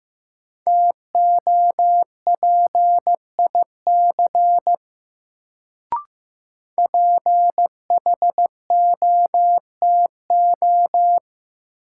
Morse code, heard as a keyed tone, about 15 wpm, 700 Hz. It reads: TOPIC PHOTO
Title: Sentences from Top 1200 Words - New Word Drill - 15wpm